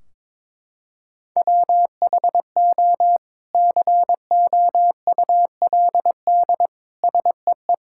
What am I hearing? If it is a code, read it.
Morse code, heard as a keyed tone, about 22 wpm, 700 Hz. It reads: WHO COULD SEE
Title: Sentences from Top 1600 Words 22wpm